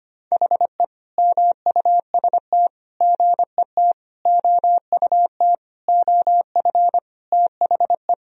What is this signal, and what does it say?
Morse code, heard as a keyed tone, about 25 wpm, 700 Hz. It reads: HE MUST GET OUT OF THE
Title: Sentences from Top 1000 Words - Review Part 1 - 25wpm